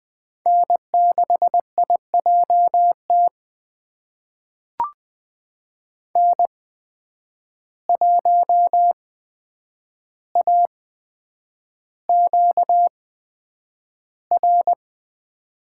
Morse code, heard as a keyed tone, about 20 wpm, spaced as for 9 wpm, 700 Hz. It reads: N6IJT N1AQR
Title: USA Call Signs - 5 Characters - 8x Character Spacing - 20wpm